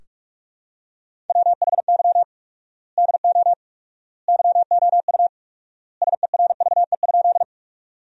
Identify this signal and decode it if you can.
Morse code, heard as a keyed tone, about 46 wpm, 700 Hz. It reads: WHY DO YOU SERVE?
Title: 4x Word-Spacing - Sentences from Top 400 Words - 45wpm